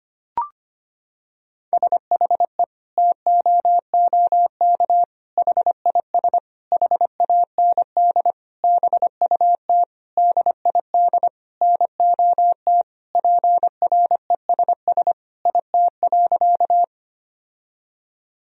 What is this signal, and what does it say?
Morse code, heard as a keyed tone, about 25 wpm, 700 Hz. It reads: SHE TOOK HIS HAND BUT DID NOT PRESS IT.
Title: Sentences from Top 1000 Words - Review Part 1 - 25wpm